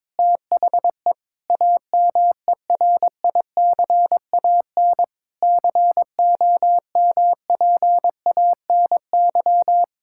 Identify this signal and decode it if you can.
Morse code, heard as a keyed tone, about 22 wpm, 700 Hz. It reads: THE AMERICAN COMPANY
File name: practice / 22wpm / Sentences with 3-Word Phrases - 22wpm